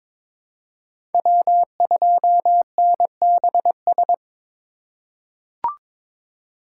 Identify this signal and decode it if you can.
Morse code, heard as a keyed tone, about 22 wpm, 700 Hz. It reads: W2NBS
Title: USA Call Signs - 5 Characters - 8x Character Spacing - 22wpm